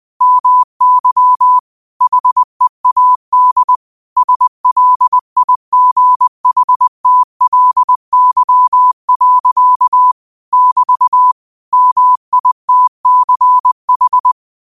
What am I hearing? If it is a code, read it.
Morse code, heard as a keyed tone, about 20 wpm, 1000 Hz. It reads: MY HEAD SLIGHTLY. <BT> MITCH
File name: qotdfiles / 20wpm